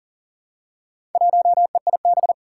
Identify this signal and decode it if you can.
Morse code, heard as a keyed tone, about 40 wpm, 700 Hz. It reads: 1EIB
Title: Four Challenging Letter-Number Combinations 40wpm